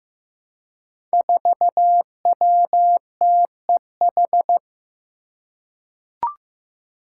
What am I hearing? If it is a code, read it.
Morse code, heard as a keyed tone, about 15 wpm, 700 Hz. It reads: 4WTEH
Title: Five Challenging Letter-Number Combinations 15wpm